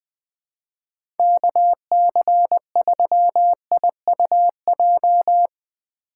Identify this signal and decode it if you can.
Morse code, heard as a keyed tone, about 20 wpm, 700 Hz. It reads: KC3IUJ
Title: USA Call Signs - 6 Characters - 2x Character Spacing - 20wpm